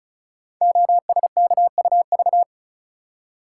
Morse code, heard as a keyed tone, about 36 wpm, 700 Hz. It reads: OSKUV
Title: Code Groups - 5 Letters - 35wpm